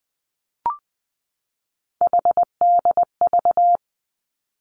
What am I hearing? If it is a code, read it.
Morse code, heard as a keyed tone, about 20 wpm, 700 Hz. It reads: HDV
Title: Three Challenging Letter-Number Combinations 20wpm